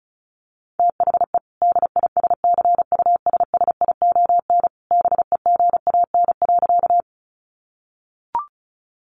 Morse code, heard as a keyed tone, about 34 wpm, 700 Hz. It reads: THE DISCUSSION BEGAN.
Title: Sentences from Top 1100 Words 35wpm